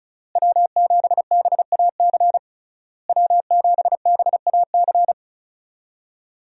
Morse code, heard as a keyed tone, about 34 wpm, 700 Hz. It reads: W7BAC W7BAC
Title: USA Call Signs - 5 Characters - Repeated 3x - 35wpm